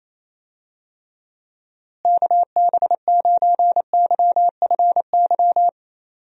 Morse code, heard as a keyed tone, about 28 wpm, 700 Hz. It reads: KB9YFY
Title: USA Call Signs - 6 Characters - Repeated 3x - 28wpm